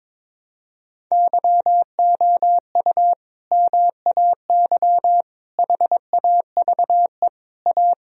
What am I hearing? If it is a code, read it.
Morse code, heard as a keyed tone, about 22 wpm, 700 Hz. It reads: YOU MAY HAVE A
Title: Sentences from Top 1000 Words - Review Part 2 - 22wpm